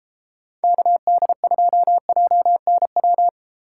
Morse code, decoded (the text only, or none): KD2JNW